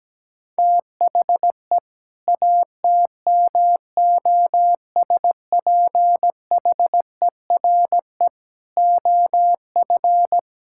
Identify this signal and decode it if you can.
Morse code, heard as a keyed tone, about 17 wpm, 700 Hz. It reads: THE ATMOSPHERE OF
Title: Sentences from Top 1600 Words 17wpm